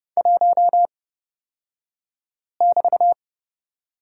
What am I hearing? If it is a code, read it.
Morse code, heard as a keyed tone, about 30 wpm, spaced as for 5 wpm, 700 Hz. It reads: 1<BT>
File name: Lesson 36d - BT - Two Characters 30wpm